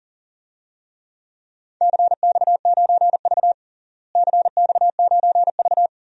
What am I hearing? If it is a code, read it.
Morse code, heard as a keyed tone, about 40 wpm, 700 Hz. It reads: CX9V CX9V
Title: DXCC-100 Call Signs - 4 Characters - Repeated 2x - 40wpm